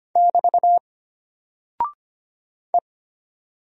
Morse code, heard as a keyed tone, about 25 wpm, 700 Hz. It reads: <BT> E
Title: Single Character (Letter:Number:Punctuation:Prosign) Alt 25wpm